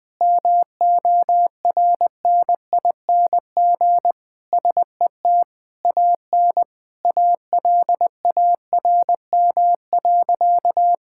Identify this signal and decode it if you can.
Morse code, heard as a keyed tone, about 20 wpm, 700 Hz. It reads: MORNING SET AN ALARM.
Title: Sentences with 8-Word Phrases - Spoken Beginning - 20wpm